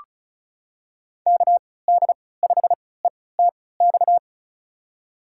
Morse code, heard as a keyed tone, about 36 wpm, spaced as for 20 wpm, 700 Hz. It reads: KD5ETX
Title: USA Call Signs - 6 Characters - 3x Character Spacing - 35wpm